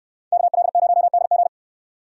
Morse code, heard as a keyed tone, about 69 wpm, 700 Hz. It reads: KL0DG